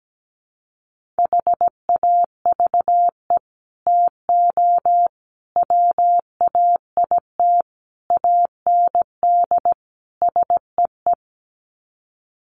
Morse code, heard as a keyed tone, about 17 wpm, 700 Hz. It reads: HAVE TO WAIT AND SEE
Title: Sets of 5 Words Encore2 17wpm